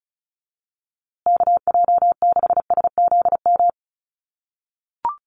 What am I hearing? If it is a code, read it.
Morse code, heard as a keyed tone, about 34 wpm, 700 Hz. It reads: KJ6SZM
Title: USA Call Signs Repeated 2x 35wpm